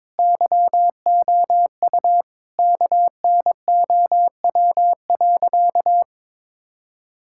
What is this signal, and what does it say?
Morse code, heard as a keyed tone, about 22 wpm, 700 Hz. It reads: YOU KNOW.
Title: Sentences from Top 1000 Words - Review Part 5 - 22wpm